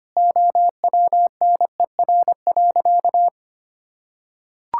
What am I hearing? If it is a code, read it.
Morse code, heard as a keyed tone, about 25 wpm, 700 Hz. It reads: OWNER.
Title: Sentences from Top 1600 Words 25wpm